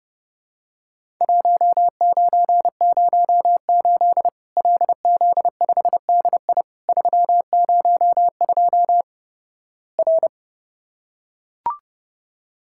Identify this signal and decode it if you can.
Morse code, heard as a keyed tone, about 30 wpm, 700 Hz. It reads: 1908 LZ5DI 302 R
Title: Worked All Europe DX Contest - QTC Exchange - 30wpm